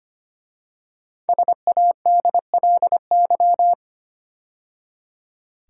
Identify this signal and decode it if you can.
Morse code, heard as a keyed tone, about 25 wpm, 700 Hz. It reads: SADLY
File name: 5 Letter Words - First 3 Letters Given - 25wpm